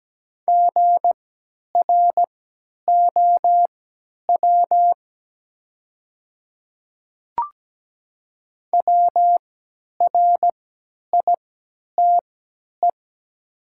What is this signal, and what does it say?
Morse code, heard as a keyed tone, about 17 wpm, spaced as for 10 wpm, 700 Hz. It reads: GROW WRITE
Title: 3x Character Spacing - Top 500 Words - 17wpm